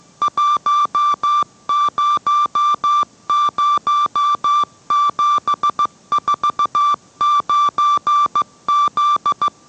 Morse code, heard as a keyed tone, about 15 wpm, 1170 Hz. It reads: 100749Z